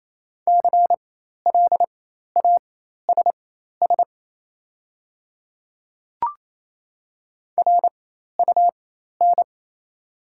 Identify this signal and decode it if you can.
Morse code, heard as a keyed tone, about 28 wpm, spaced as for 13 wpm, 690 Hz. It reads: CLASS RUN
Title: 4x Character Spacing - Top 500 Words - 28wpm